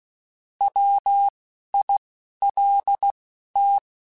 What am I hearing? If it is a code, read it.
Morse code, heard as a keyed tone, about 16 wpm, 800 Hz. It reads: W I L T
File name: Alphabetical / Alphabetical mp3s / Ws @10